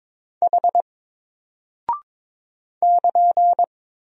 Morse code, heard as a keyed tone, about 22 wpm, 700 Hz. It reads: H <KN>
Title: Single Character (Letter:Number:Punctuation:Prosign) Alt 22wpm